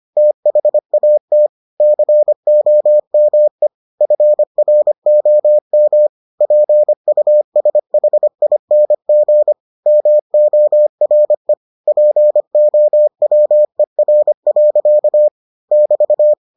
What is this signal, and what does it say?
Morse code, heard as a keyed tone, about 25 wpm, 600 Hz. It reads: THAT COME FROM PUSHING MORE POWER. <BT>